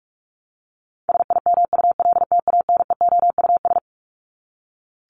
Morse code, heard as a keyed tone, about 44 wpm, 700 Hz. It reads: SIMULTANEOUS
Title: LY Suffix - Top 100 Words with Base Words 45wpm